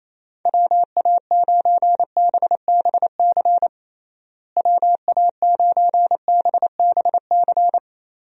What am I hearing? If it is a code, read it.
Morse code, heard as a keyed tone, about 28 wpm, 700 Hz. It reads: WA9BBC WA9BBC
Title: USA Call Signs - 6 Characters - Repeated 2x - 28wpm